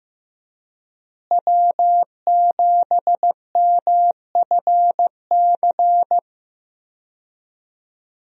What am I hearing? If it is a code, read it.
Morse code, heard as a keyed tone, about 16 wpm, 700 Hz. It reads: W7MFC
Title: USA Call Signs - 5 Characters - Repeated 3x - 15wpm